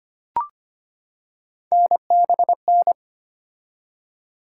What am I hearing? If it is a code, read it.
Morse code, heard as a keyed tone, about 25 wpm, 700 Hz. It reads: NBN
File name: Three Letter-Number Combinations - 25wpm